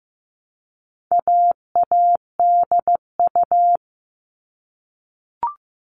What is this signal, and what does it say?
Morse code, heard as a keyed tone, about 15 wpm, 700 Hz. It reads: AADU
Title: All Cut Numbers - 4-Digit 15wpm